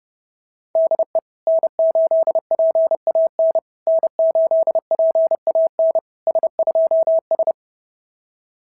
Morse code, heard as a keyed tone, about 28 wpm, 660 Hz. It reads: DE N8PAN N8PAN S2S
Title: SOTA QSO Practice 30wpm